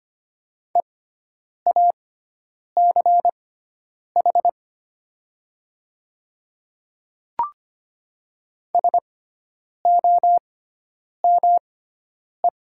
Morse code, heard as a keyed tone, about 25 wpm, spaced as for 9 wpm, 700 Hz. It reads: EACH SOME
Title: 6x Character Spacing - Top 100 Words - 25wpm